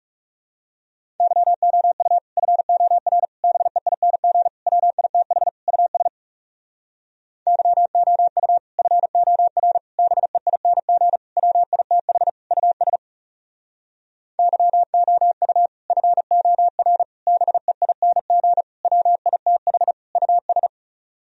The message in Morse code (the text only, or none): YOU FOR BEING WITH US YOU FOR BEING WITH US YOU FOR BEING WITH US